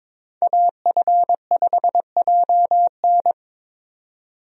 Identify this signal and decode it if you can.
Morse code, heard as a keyed tone, about 22 wpm, 700 Hz. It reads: AF5JN
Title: USA Call Signs - 5 Characters - 22wpm